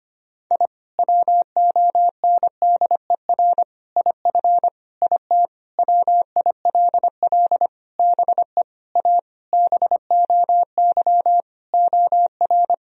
Morse code, heard as a keyed tone, about 25 wpm, 700 Hz. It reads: I WONDER IF IT WILL BE A BOY OR
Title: Sentences from Top 1000 Words - Review Part 3 - 25wpm